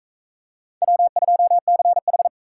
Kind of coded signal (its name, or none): Morse code